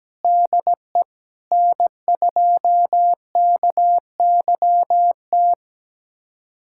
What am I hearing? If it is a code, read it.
Morse code, heard as a keyed tone, about 17 wpm, 700 Hz. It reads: DE N2KYT
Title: USA Callsign DE Callsign 17wpm